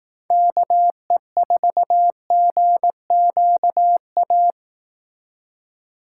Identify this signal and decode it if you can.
Morse code, heard as a keyed tone, about 18 wpm, 700 Hz. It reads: KE4GQA
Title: Speed Racing - USA 2x3 Call Signs 15wpm